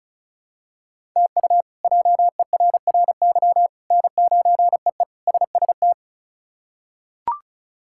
Morse code, heard as a keyed tone, about 36 wpm, 700 Hz. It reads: TU JERRY N9EE SST
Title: K1USN SST QSO Practice - 35wpm